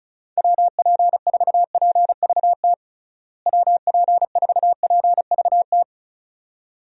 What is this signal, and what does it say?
Morse code, heard as a keyed tone, about 36 wpm, 700 Hz. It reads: WP4PVT WP4PVT
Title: USA Call Signs - 6 Characters - Repeated 2x - 35wpm